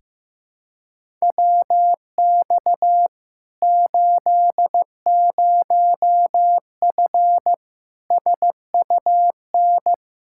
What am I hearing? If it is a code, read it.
Morse code, heard as a keyed tone, about 15 wpm, 700 Hz. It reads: WX 80F SUN
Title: Practice QSO - Exchange 2 - Info - 15wpm